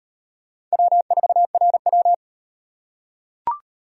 Morse code, heard as a keyed tone, about 38 wpm, 700 Hz. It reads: W4RW